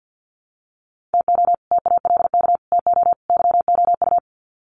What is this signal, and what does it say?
Morse code, heard as a keyed tone, about 48 wpm, 700 Hz. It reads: TO TALK TO YOU
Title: Sets of 4 Words 50wpm